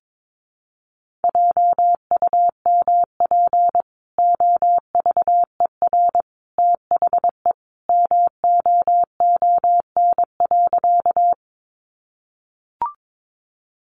Morse code, heard as a keyed tone, about 22 wpm, 700 Hz. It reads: JUMP OVER THE MOON.
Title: Sentences from Top 1300 Words 22wpm